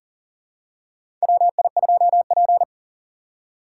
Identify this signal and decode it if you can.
Morse code, heard as a keyed tone, about 40 wpm, 700 Hz. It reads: WI2P
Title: USA Call Signs - 4x Character Spacing - 40wpm